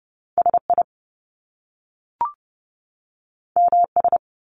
Morse code, heard as a keyed tone, about 30 wpm, 700 Hz. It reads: SI MS
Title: Top Two Letter Combinations in Words 30wpm